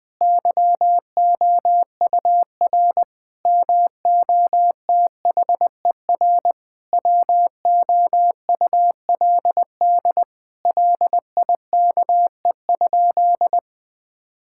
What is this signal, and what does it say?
Morse code, heard as a keyed tone, about 20 wpm, 700 Hz. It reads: YOUR MOTHER WOULD LIKE?
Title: Sentences from Top 1600 Words 20wpm